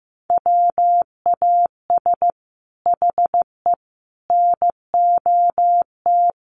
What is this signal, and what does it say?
Morse code, heard as a keyed tone, about 15 wpm, 700 Hz. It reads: WAS HE NOT
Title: Sentences from Top 1400 Words 15wpm